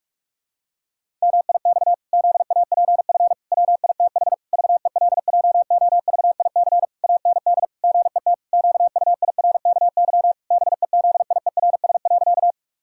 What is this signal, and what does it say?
Morse code, heard as a keyed tone, about 46 wpm, 700 Hz. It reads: MIX ZAPF WITH VELJOVIC AND GET QUIRKY BEZIERS.
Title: Pangram Challenge v2 - 45wpm